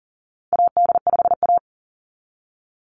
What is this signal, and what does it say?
Morse code, heard as a keyed tone, about 40 wpm, 700 Hz. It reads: AD5A
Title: NA SOTA Chaser Callsigns 40wpm